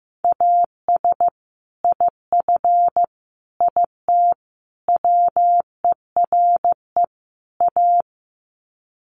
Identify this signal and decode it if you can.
Morse code, heard as a keyed tone, about 15 wpm, 700 Hz. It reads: AS IF IT WERE A
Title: Sets of 5 Words 15wpm